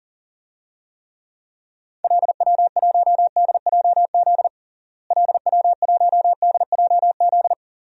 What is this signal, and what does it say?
Morse code, heard as a keyed tone, about 40 wpm, 700 Hz. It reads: LW1DJZ LW1DJZ
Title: DXCC-100 Call Signs - 6 Characters - Repeated 2x - 40wpm